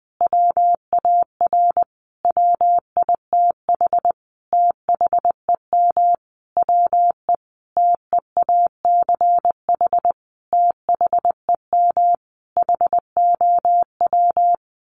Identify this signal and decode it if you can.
Morse code, heard as a keyed tone, about 19 wpm, 700 Hz. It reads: WAR WITH THEM WE TEACH THEM HOW
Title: Sentences from Top 1300 Words 20wpm